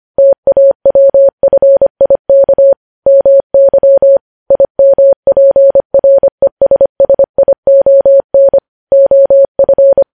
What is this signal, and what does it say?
Morse code, heard as a keyed tone, about 25 wpm, 550 Hz. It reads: TAWFIK MY IMPRESSION OF
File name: podcast / News headlines-1x-25x25